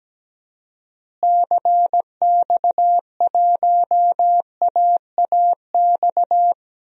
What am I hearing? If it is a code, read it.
Morse code, heard as a keyed tone, about 17 wpm, 700 Hz. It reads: CX1AAX
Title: DXCC-100 Call Signs - 6 Characters - Repeated 3x - 17wpm